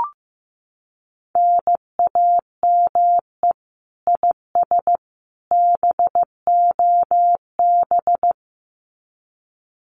Morse code, heard as a keyed tone, about 15 wpm, 700 Hz. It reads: NAME IS BOB
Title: Common QSO Phrases 15wpm